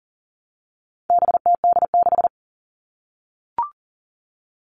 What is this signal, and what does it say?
Morse code, heard as a keyed tone, about 40 wpm, 700 Hz. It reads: BTD6
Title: All Cut Numbers - 4-Digit 40wpm